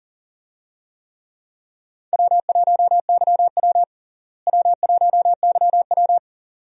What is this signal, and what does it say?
Morse code, heard as a keyed tone, about 40 wpm, 700 Hz. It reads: W1YW W1YW
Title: USA Call Signs - 4 Characters - Repeated 2x - 40wpm